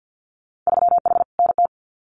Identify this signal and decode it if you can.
Morse code, heard as a keyed tone, about 50 wpm, 700 Hz. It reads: 3H NT
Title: Winter Field Day Exchanges Encore 50wpm